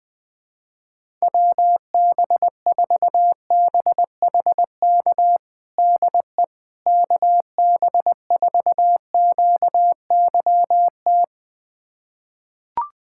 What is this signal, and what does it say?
Morse code, heard as a keyed tone, about 20 wpm, 700 Hz. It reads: WB4BHK DE KB4QYT
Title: USA Callsign DE Callsign - 6 Characters - 20wpm